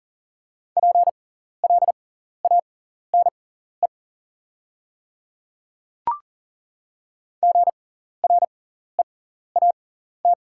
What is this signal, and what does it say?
Morse code, heard as a keyed tone, about 39 wpm, spaced as for 14 wpm, 700 Hz. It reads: PLANE GREAT
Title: 6x Character Spacing - Top 500 Words - 40wpm